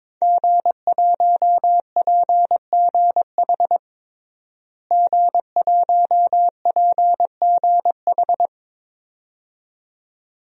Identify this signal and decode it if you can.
Morse code, heard as a keyed tone, about 22 wpm, 700 Hz. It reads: G1PGH G1PGH
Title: DXCC-100 Call Signs - 5 Characters - Repeated 2x - 22wpm